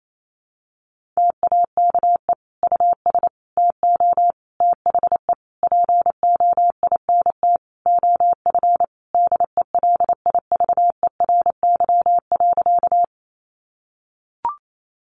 Morse code, heard as a keyed tone, about 28 wpm, 700 Hz. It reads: TAKE US TO THE POINT OF DELIVERY.
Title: Sentences from Top 1500 Words 28wpm